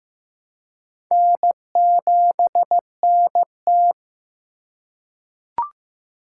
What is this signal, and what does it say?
Morse code, heard as a keyed tone, about 15 wpm, 700 Hz. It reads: N7NT
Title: K1USN SST QSO Practice - 15wpm